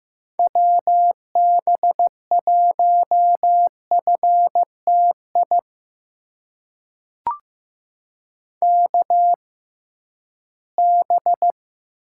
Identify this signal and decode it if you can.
Morse code, heard as a keyed tone, about 15 wpm, spaced as for 11 wpm, 700 Hz. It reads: WB1FTI KB
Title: USA Call Signs - 6 Characters - 6x Character Spacing - 15wpm